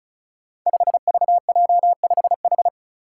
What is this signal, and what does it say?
Morse code, heard as a keyed tone, about 36 wpm, 700 Hz. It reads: 5VJ5H
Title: Five Challenging Letter-Number Combinations 35wpm